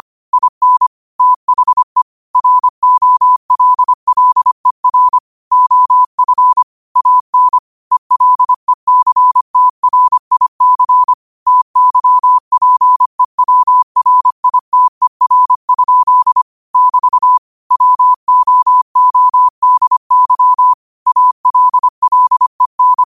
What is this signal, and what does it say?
Morse code, heard as a keyed tone, about 25 wpm, 1000 Hz. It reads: IN THE ROLLER OF AN ELECTRIC TYPEWRITER? <BT> WOODY ALLEN
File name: 25wpm